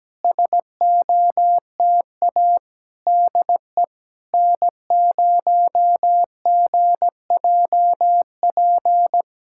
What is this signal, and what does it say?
Morse code, heard as a keyed tone, about 17 wpm, 700 Hz. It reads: SOTA DE N0GJP
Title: SOTA QSO Practice 17wpm